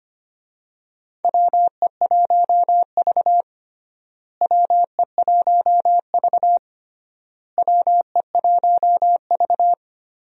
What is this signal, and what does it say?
Morse code, heard as a keyed tone, about 25 wpm, 700 Hz. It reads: WE1V WE1V WE1V
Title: USA Call Signs - 4 Characters - Repeated 3x - 25wpm